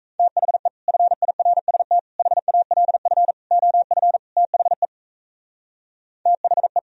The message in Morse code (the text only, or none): THE FIRST HALF OF THE THE